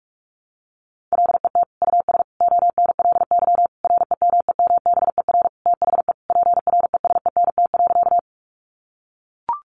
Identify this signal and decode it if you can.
Morse code, heard as a keyed tone, about 44 wpm, 700 Hz. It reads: LET US ONLY REMEMBER THE PRESENT.
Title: Sentences from Top 800 Words 45wpm